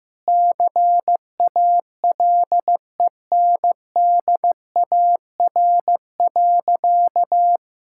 Morse code, heard as a keyed tone, about 16 wpm, 700 Hz. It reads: CALENDAR.
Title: Sentences from Top 1600 Words 15wpm